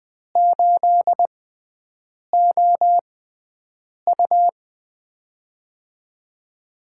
Morse code, heard as a keyed tone, about 20 wpm, spaced as for 7 wpm, 700 Hz. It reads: 8OU
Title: USA Call Signs - 4 Characters - 6x Character Spacing - 20wpm